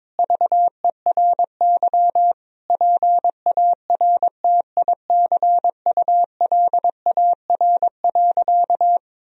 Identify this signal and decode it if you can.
Morse code, heard as a keyed tone, about 22 wpm, 700 Hz. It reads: VERY PARTICULAR.
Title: Sentences from Top 1000 Words - Review Part 4 - 22wpm